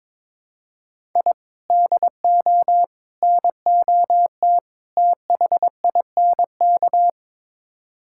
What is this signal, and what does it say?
Morse code, heard as a keyed tone, about 21 wpm, 700 Hz. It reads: I DO NOT THINK
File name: practice / 22wpm / Build A Phrase - 5-Word Phrases - 22wpm